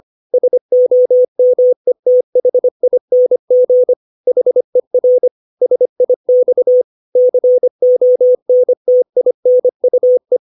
Morse code, heard as a keyed tone, about 25 wpm, 500 Hz. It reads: SOMETHING HER SIX CONTINUE
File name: 632 words / 632 Words 25WPM